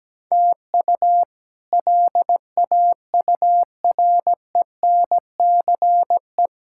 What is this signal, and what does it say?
Morse code, heard as a keyed tone, about 17 wpm, 700 Hz. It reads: TU LAURENCE
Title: K1USN SST QSO Practice - 17wpm